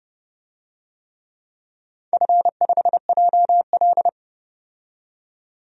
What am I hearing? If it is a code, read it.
Morse code, heard as a keyed tone, about 30 wpm, 700 Hz. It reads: F5JL
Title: DXCC-100 Call Signs - 4 Characters - 30wpm